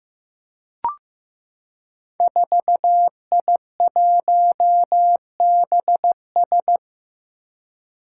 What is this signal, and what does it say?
Morse code, heard as a keyed tone, about 15 wpm, 700 Hz. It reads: 4I1BS
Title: Five Challenging Letter-Number Combinations 15wpm